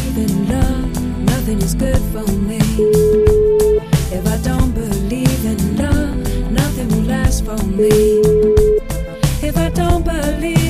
Ringback tone (Europe, 425 Hz), with noise.